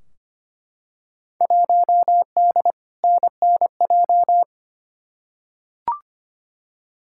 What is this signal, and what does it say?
Morse code, heard as a keyed tone, about 25 wpm, 700 Hz. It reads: 1D NNJ
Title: Field Day Exchanges Encore - 25wpm